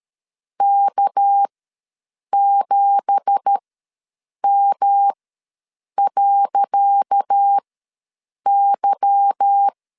Morse code, heard as a keyed tone, about 13 wpm, spaced as for 7 wpm, 800 Hz. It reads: K7M.Y
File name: LEZIONI / 10 SIGNS / 10C DOT ws 12wpm 3548